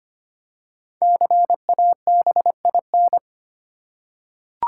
Morse code, heard as a keyed tone, about 25 wpm, 700 Hz. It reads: CABIN
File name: 5 Letter Words - First 3 Letters Given - 25wpm